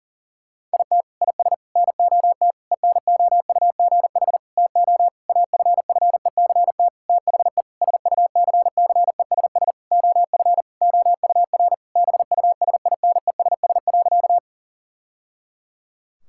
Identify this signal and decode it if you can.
Morse code, heard as a keyed tone, about 40 wpm, 700 Hz. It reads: IT IS NOT ENOUGH TO AFFECT THE SUCCESS OF OUR BUSINESS.
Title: Sentences from Top 1000 Words - Review Part 6 - 40wpm